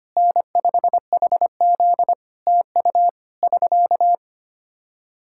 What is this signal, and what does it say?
Morse code, heard as a keyed tone, about 25 wpm, 700 Hz. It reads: N5HZ TU <SK>
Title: Practice QSO - Exchange 4 - Ending - 25wpm